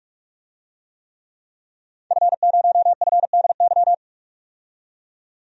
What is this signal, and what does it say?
Morse code, heard as a keyed tone, about 44 wpm, 700 Hz. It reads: F0FDY
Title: DXCC-100 Call Signs - 5 Characters - 45wpm